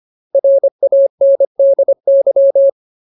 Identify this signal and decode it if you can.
Morse code, heard as a keyed tone, about 25 wpm, 550 Hz. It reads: RANDY
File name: Ultimate Word List / 25WPM Ultimate